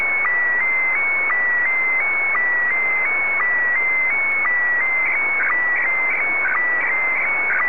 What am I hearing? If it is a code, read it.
SSTV (slow-scan television), Scottie DX.
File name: sstv / SDX-VK3AOB